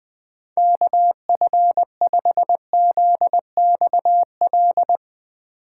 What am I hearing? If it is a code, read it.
Morse code, heard as a keyed tone, about 20 wpm, 700 Hz. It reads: KF5ZXL
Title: USA Call Signs - 2x Character Spacing - 20wpm